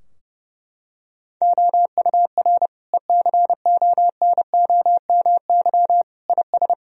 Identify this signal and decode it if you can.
Morse code, heard as a keyed tone, about 30 wpm, 700 Hz. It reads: OUR ECONOMY IS
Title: Sentences from Top 1000 Words - Review Part 3 - 30wpm